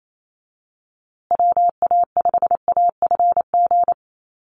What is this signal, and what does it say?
Morse code, heard as a keyed tone, about 28 wpm, 700 Hz. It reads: WA5AFG
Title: USA Call Signs Repeated 2x 28wpm